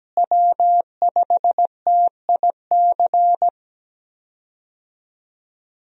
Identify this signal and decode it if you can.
Morse code, heard as a keyed tone, about 17 wpm, 700 Hz. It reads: W5TIC
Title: USA Call Signs - 5 Characters - Repeated 2x - 17wpm